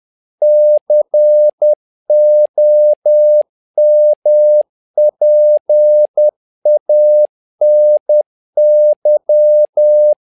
Morse code, heard as a keyed tone, about 10 wpm, 600 Hz. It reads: COMPANY
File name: learning center / tower trouble / tower trouble 10wpm